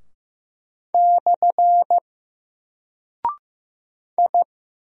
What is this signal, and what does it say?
Morse code, heard as a keyed tone, about 15 wpm, 700 Hz. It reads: / I
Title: Single Character (Letter:Number:Punctuation:Prosign) 15wpm